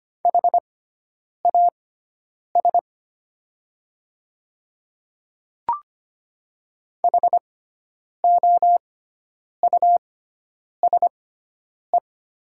Morse code, heard as a keyed tone, about 25 wpm, spaced as for 9 wpm, 700 Hz. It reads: HAS HOUSE
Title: 6x Character Spacing - Top 200 Words - 25wpm